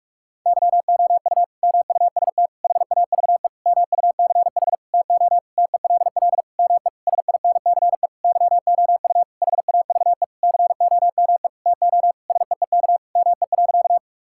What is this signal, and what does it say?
Morse code, heard as a keyed tone, about 46 wpm, 700 Hz. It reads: YOU MUST HAVE MUCH TO TELL ME SINCE YOU HAVE COME TO SEEK ME.
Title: Sentences from Top 1600 Words 45wpm